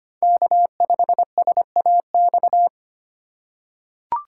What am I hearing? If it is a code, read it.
Morse code, heard as a keyed tone, about 25 wpm, 700 Hz. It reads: K5SAX
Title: USA Call Signs - 5 Characters - 8x Character Spacing - 25wpm